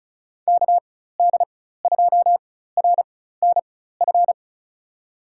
Morse code, heard as a keyed tone, about 36 wpm, spaced as for 16 wpm, 700 Hz. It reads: KD2RNF
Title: USA Call Signs - 6 Characters - 4x Character Spacing - 35wpm